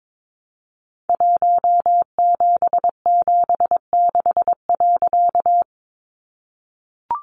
Morse code, heard as a keyed tone, about 22 wpm, 700 Hz. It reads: 1776.
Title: A Sound Of Thunder 22wpm